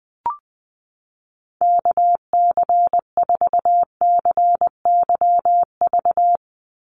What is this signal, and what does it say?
Morse code, heard as a keyed tone, about 20 wpm, 700 Hz. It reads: KC4CYV
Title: USA Call Signs 20wpm